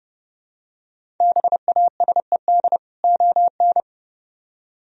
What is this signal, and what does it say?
Morse code, heard as a keyed tone, about 28 wpm, 700 Hz. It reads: BASED ON